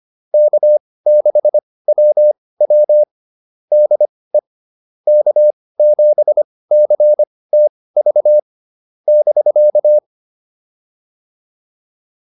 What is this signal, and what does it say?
Morse code, heard as a keyed tone, about 25 wpm, spaced as for 18 wpm, 600 Hz. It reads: K6WW DE K7CTV <BK>
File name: QSO204 18